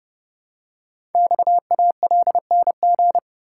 Morse code, heard as a keyed tone, about 30 wpm, 700 Hz. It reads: XALNG
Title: Code Groups - 5 Letters - 30wpm